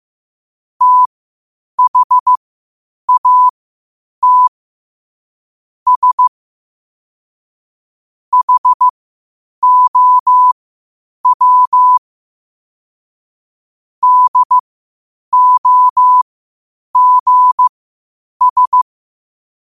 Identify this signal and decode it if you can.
Morse code, heard as a keyed tone, about 15 wpm, spaced as for 8 wpm, 1000 Hz. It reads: THAT S HOW DOGS
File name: qotdfiles / 8wpm